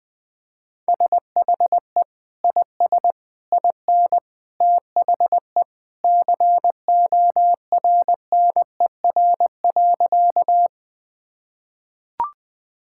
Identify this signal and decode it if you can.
Morse code, heard as a keyed tone, about 20 wpm, 700 Hz. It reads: SHE IS IN THE CORNER.
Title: Sentences from Top 1600 Words 20wpm